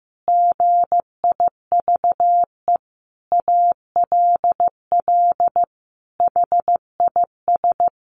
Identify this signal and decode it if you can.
Morse code, heard as a keyed tone, about 14 wpm, 700 Hz. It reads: GIVE ALL HIS
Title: Sentences from Top 1500 Words 15wpm